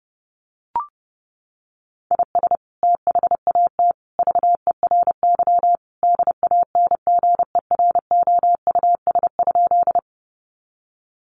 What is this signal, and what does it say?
Morse code, heard as a keyed tone, about 30 wpm, 700 Hz. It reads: IS THAT VERY DANGEROUS?
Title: Sentences from Top 1200 Words 30wpm